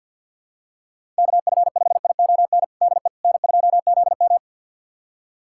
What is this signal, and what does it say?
Morse code, heard as a keyed tone, about 51 wpm, 700 Hz. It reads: KF5ION DE N2ZM
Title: USA Callsign DE Callsign 50wpm